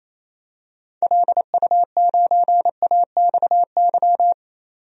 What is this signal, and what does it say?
Morse code, heard as a keyed tone, about 27 wpm, 700 Hz. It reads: LU9AXY
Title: DXCC-100 Call Signs - 6 Characters - 28wpm